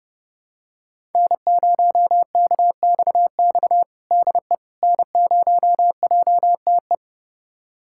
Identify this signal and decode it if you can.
Morse code, heard as a keyed tone, about 30 wpm, 700 Hz. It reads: N0KXX DE N0JTE
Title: USA Callsign DE Callsign - 5 Characters - 30wpm